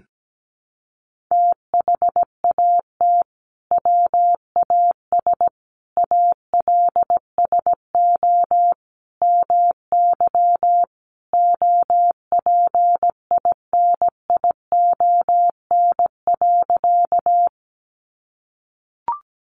Morse code, heard as a keyed tone, about 17 wpm, 700 Hz. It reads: THAT WAS ALSO MY OPINION.
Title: Sentences from Top 1200 Words 17wpm